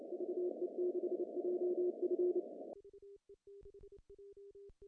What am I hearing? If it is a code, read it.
Morse code, heard as a keyed tone, about 30 wpm, 360 Hz. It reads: VE6JF
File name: CW / cqww-cw-k5tr